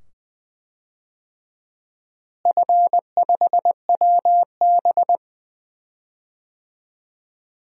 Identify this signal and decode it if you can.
Morse code, heard as a keyed tone, about 20 wpm, 700 Hz. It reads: F5WB
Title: DXCC-100 Call Signs - 4 Characters - Repeated 2x - 20wpm